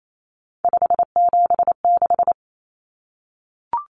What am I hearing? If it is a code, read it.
Morse code, heard as a keyed tone, about 28 wpm, 700 Hz. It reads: 576
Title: Numbers - 3-Digit 28wpm